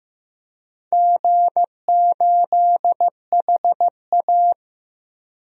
Morse code, heard as a keyed tone, about 15 wpm, 700 Hz. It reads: G8HA
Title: DXCC-100 Call Signs - 4 Characters - 15wpm